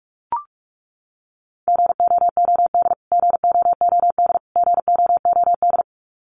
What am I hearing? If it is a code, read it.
Morse code, heard as a keyed tone, about 44 wpm, 700 Hz. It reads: GOOD GOOD GOOD
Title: Top 500 Words Repeated 45wpm